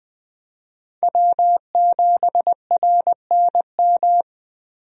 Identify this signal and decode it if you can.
Morse code, heard as a keyed tone, about 20 wpm, 700 Hz. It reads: W7RNM
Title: USA Call Signs - 5 Characters - 20wpm